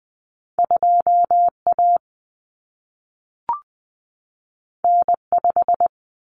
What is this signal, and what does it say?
Morse code, heard as a keyed tone, about 20 wpm, 700 Hz. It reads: 2A N5
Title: Common Cut Numbers - 2-Digit 20wpm